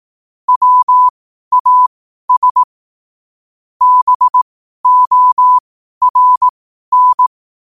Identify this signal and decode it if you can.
Morse code, heard as a keyed tone, about 18 wpm, spaced as for 12 wpm, 1000 Hz. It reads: WAS BORN
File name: qotdfiles / 12wpm